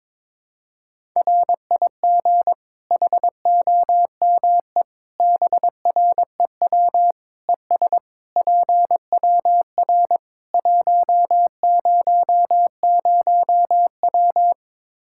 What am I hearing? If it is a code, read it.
Morse code, heard as a keyed tone, about 22 wpm, 700 Hz. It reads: RIG HOME BREW ES PWR 100W
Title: Practice QSO - Exchange 2 - Info - 22wpm